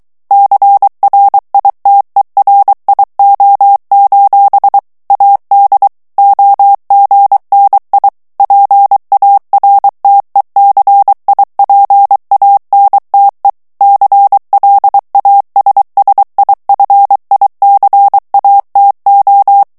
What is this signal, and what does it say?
Morse code, heard as a keyed tone, about 23 wpm, 800 Hz. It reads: CRITERIO: AD OGNI PARTECIPANTE CLASSIFICATO
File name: ITALIEN texte clair 2 vitesse 22 mots minute